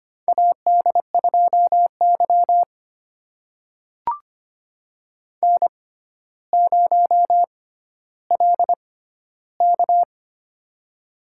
Morse code, heard as a keyed tone, about 25 wpm, spaced as for 13 wpm, 700 Hz. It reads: AD2Y N0LK
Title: USA Call Signs - 4 Characters - 6x Character Spacing - 25wpm